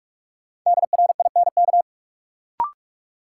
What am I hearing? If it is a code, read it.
Morse code, heard as a keyed tone, about 46 wpm, 700 Hz. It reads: DRINK
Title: Sentences from Top 1500 Words - New Word Drill - 45wpm